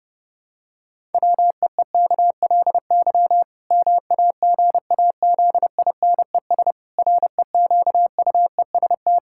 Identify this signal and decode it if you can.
Morse code, heard as a keyed tone, about 32 wpm, 700 Hz. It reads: WEEKLY MAGAZINES REQUEST
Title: Pangram Challenge v2 - 30wpm